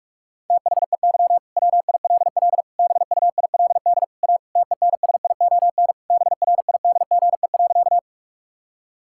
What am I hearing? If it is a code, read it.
Morse code, heard as a keyed tone, about 41 wpm, 700 Hz. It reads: THEY WILL BUILD A TENSION BRIDGE.
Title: Sentences from Top 1600 Words 45wpm